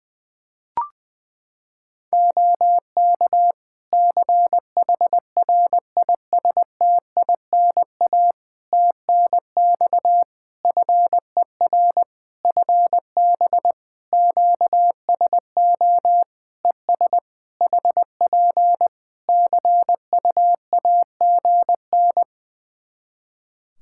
Morse code, heard as a keyed tone, about 20 wpm, 700 Hz. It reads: OK CHRISTINA TNX FER FB QSO ES HP CUAGN
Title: Practice QSO - Exchange 4 - Ending - 20wpm